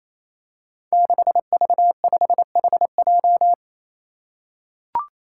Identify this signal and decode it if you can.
Morse code, heard as a keyed tone, about 28 wpm, 700 Hz. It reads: 6V5HJ
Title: Five Challenging Letter-Number Combinations 28wpm